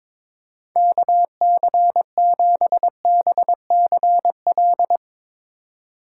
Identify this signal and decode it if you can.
Morse code, heard as a keyed tone, about 21 wpm, 700 Hz. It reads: KC7BCL
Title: USA Call Signs - 6 Characters - 2x Character Spacing - 22wpm